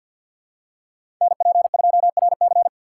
Morse code, heard as a keyed tone, about 52 wpm, 700 Hz. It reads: NP2RC
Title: USA Call Signs - 5 Characters - 8x Character Spacing - 50wpm